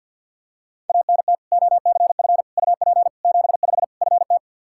Morse code, heard as a keyed tone, about 52 wpm, 700 Hz. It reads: ANT OCF UP 75 FT
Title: Practice QSO - Exchange 2 - Info - 50wpm